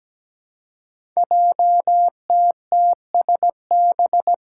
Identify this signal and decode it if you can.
Morse code, heard as a keyed tone, about 17 wpm, 700 Hz. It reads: JTTSB
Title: Five Challenging Letter-Number Combinations 17wpm